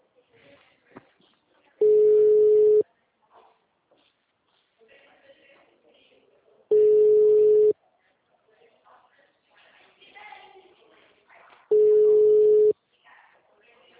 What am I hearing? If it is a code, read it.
Ringback tone (Europe, 425 Hz), with speech.